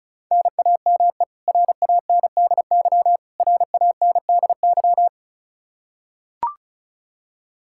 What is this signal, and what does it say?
Morse code, heard as a keyed tone, about 36 wpm, 700 Hz. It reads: NAME RANDY RANDY
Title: Practice QSO - Exchange 1 - Beginning - 35wpm